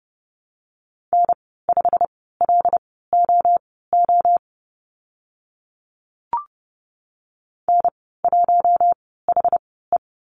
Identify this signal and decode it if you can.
Morse code, heard as a keyed tone, about 32 wpm, spaced as for 17 wpm, 690 Hz. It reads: N5LOO N1HE
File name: Lesson 15g - Number 5 - Call Signs 30wpm